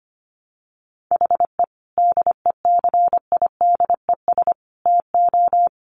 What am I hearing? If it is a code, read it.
Morse code, heard as a keyed tone, about 25 wpm, 700 Hz. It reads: HE DECIDES TO
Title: Additional Easy Phrases 25wpm